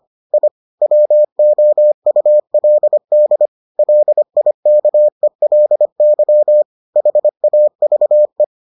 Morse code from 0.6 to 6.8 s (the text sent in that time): WOULD LIKELY